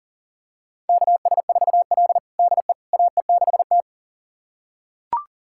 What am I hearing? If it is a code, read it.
Morse code, heard as a keyed tone, about 37 wpm, 700 Hz. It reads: KS4L DE AE6T
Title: USA Callsign DE Callsign - 4 Characters - 40wpm